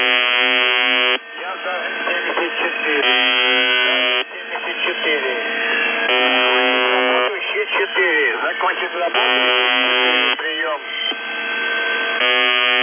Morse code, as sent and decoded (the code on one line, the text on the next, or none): none